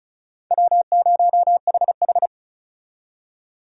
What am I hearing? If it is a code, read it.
Morse code, heard as a keyed tone, about 35 wpm, 700 Hz. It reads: W0HH
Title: USA Call Signs - 4 Characters - 35wpm